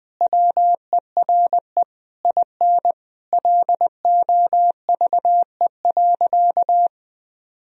Morse code, heard as a keyed tone, about 20 wpm, 700 Hz. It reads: WERE IN LOVE.
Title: Sentences from Top 1000 Words - Review Part 4 - 20wpm